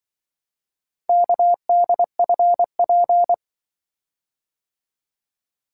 Morse code, heard as a keyed tone, about 23 wpm, 700 Hz. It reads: KDFP
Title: Speed-Racing - Four Letter-Number Combinations - 20wpm